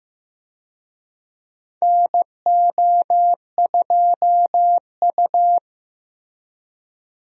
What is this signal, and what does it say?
Morse code, heard as a keyed tone, about 15 wpm, 700 Hz. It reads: NO2U
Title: USA Call Signs - 4 Characters - Repeated 3x - 15wpm